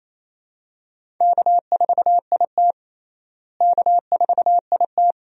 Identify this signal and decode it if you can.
Morse code, heard as a keyed tone, about 28 wpm, 700 Hz. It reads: K4IT K4IT
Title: USA Call Signs - 4 Characters - Repeated 3x - 28wpm